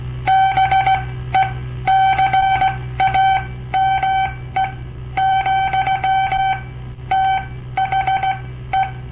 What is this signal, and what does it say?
Morse code, heard as a keyed tone, about 17 wpm, spaced as for 13 wpm, 770 Hz. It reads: BECAME, THE
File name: papertapes / local / Code Practice-01